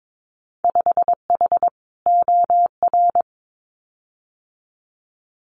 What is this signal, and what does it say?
Morse code, heard as a keyed tone, about 22 wpm, 700 Hz. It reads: 5H OR
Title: Winter Field Day Exchanges Encore 22wpm